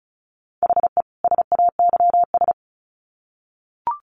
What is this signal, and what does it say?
Morse code, heard as a keyed tone, about 34 wpm, 700 Hz. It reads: HE SAYS
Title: Sets of 2 Words 35wpm